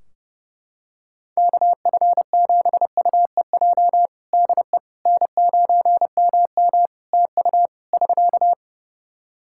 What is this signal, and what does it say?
Morse code, heard as a keyed tone, about 30 wpm, 700 Hz. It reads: KF7UEJ DE N9MM TU <SK>
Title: Practice QSO - Exchange 4 - Ending - 30wpm